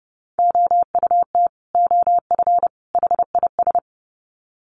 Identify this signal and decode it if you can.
Morse code, heard as a keyed tone, about 30 wpm, 700 Hz. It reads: OUT OF HIS
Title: Sets of 3 Words 30wpm